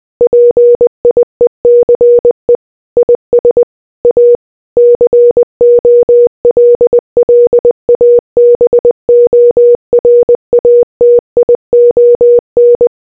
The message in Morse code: PIECE IS A COLLABORATION